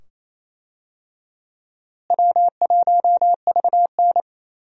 Morse code, heard as a keyed tone, about 28 wpm, 700 Hz. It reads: W1VN
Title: USA Call Signs - 4 Characters - Repeated 2x - 28wpm